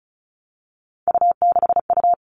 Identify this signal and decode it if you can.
Morse code, heard as a keyed tone, about 34 wpm, 700 Hz. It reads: U6U
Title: All Cut Numbers - 3-Digit 35wpm